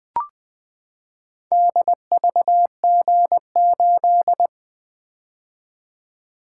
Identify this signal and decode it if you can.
Morse code, heard as a keyed tone, about 20 wpm, 700 Hz. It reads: DVG8
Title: Four Letter-Number Combinations - 20wpm